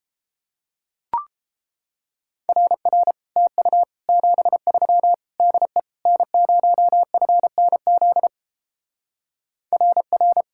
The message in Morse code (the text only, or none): RR TU 73 DE N0FNZ RR